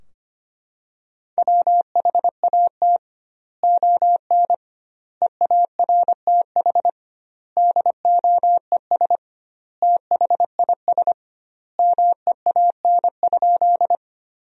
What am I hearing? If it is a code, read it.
Morse code, heard as a keyed tone, about 25 wpm, 700 Hz. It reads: WHAT ON EARTH DOES THIS MEAN?
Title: 2x Word-Spacing - Sentences from Top 200 Words - 25wpm